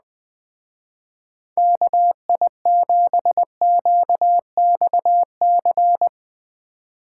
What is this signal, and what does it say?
Morse code, heard as a keyed tone, about 19 wpm, 700 Hz. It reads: KI7QXC
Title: Speed Racing - USA 2x3 Call Signs 17wpm